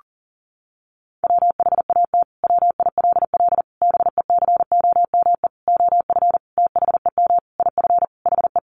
Morse code, heard as a keyed tone, about 40 wpm, 700 Hz. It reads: WHAT WILL BECOME OF THEM IF HE
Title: Sentences from Top 1100 Words 40wpm